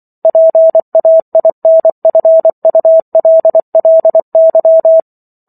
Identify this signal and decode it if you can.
Morse code, heard as a keyed tone, about 24 wpm, 650 Hz. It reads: PAINFULLY